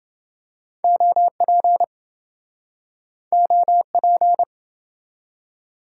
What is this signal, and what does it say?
Morse code, heard as a keyed tone, about 25 wpm, 700 Hz. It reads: OP OP